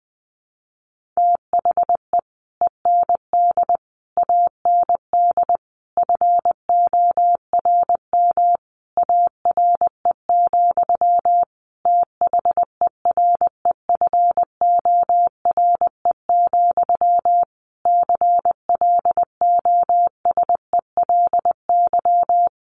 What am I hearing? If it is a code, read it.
Morse code, heard as a keyed tone, about 20 wpm, 700 Hz. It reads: THE END AND FORM ARE, THEREFORE, CLOSELY
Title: Sentences from Top 1400 Words 20wpm